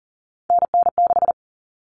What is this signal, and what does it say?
Morse code, heard as a keyed tone, about 40 wpm, 700 Hz. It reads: NN6
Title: Common Cut Numbers - 3-Digit 40wpm